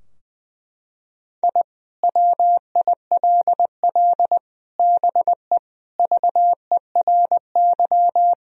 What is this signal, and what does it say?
Morse code, heard as a keyed tone, about 21 wpm, 700 Hz. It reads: I WILL BE VERY
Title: Sentences from Top 1000 Words - Review Part 3 - 20wpm